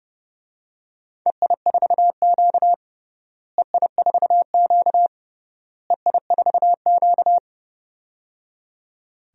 Morse code, heard as a keyed tone, about 30 wpm, 700 Hz. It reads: EI4Q EI4Q EI4Q
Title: DXCC-100 Call Signs - 4 Characters - Repeated 3x - 30wpm